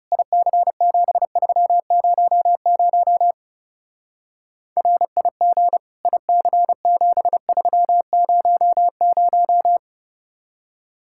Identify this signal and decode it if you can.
Morse code, heard as a keyed tone, about 36 wpm, 700 Hz. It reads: IC7300 RIG IC7300
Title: Speed-Racing Common QSO Phrases - 30wpm